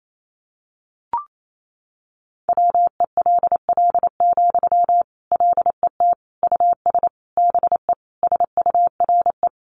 Morse code, heard as a keyed tone, about 28 wpm, 700 Hz. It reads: WELL, LET US BE SURE
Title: Sentences from Top 300 Words 28wpm